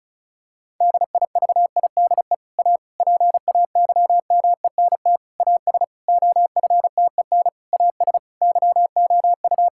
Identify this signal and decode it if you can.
Morse code, heard as a keyed tone, about 36 wpm, 700 Hz. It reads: DIVIDE A PAYMENT AS OFTEN AS YOU
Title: Sentences from Top 1600 Words 35wpm